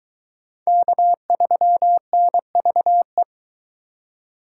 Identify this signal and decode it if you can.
Morse code, heard as a keyed tone, about 22 wpm, 700 Hz. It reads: K3NVE